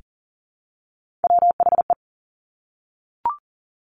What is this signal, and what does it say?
Morse code, heard as a keyed tone, about 42 wpm, 700 Hz. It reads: WHE
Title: Top Three Letter Combinations in Words 40wpm